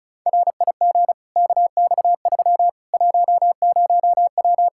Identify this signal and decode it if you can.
Morse code, heard as a keyed tone, about 36 wpm, 700 Hz. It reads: RIG KX3 10W
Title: Practice QSO - Exchange 2 - Info - 35wpm